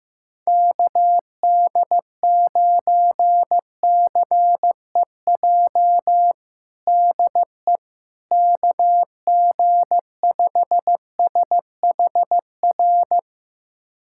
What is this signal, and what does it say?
Morse code, heard as a keyed tone, about 15 wpm, 700 Hz. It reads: KD9CEJ DE KG5SHR
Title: USA Callsign DE Callsign - 6 Characters - 15wpm